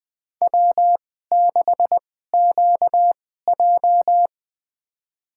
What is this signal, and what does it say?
Morse code, heard as a keyed tone, about 20 wpm, spaced as for 15 wpm, 700 Hz. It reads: W6QJ
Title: USA Call Signs - 4 Characters - 2x Character Spacing - 20wpm